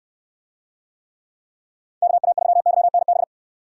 Morse code, heard as a keyed tone, about 62 wpm, 700 Hz. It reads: KA3YNB